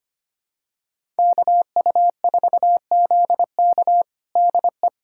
Morse code, heard as a keyed tone, about 25 wpm, 700 Hz. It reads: KU4ZK DE
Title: USA Callsign DE Callsign - 5 Characters - 25wpm